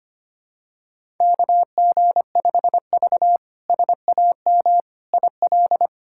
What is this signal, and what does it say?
Morse code, heard as a keyed tone, about 25 wpm, 700 Hz. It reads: KG5V SAM IL
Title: K1USN SST QSO Practice - 25wpm